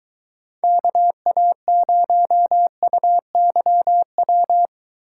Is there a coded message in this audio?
Yes: Morse code from 0.6 to 4.7 s, heard as a keyed tone, about 23 wpm, 700 Hz. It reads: KA0UYW